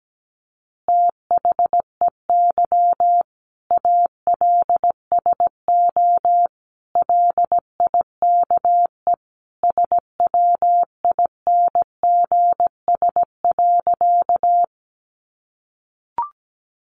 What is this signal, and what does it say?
Morse code, heard as a keyed tone, about 17 wpm, 700 Hz. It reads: THEY ALSO LIKE SWINGS.
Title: Easy Phrases 17wpm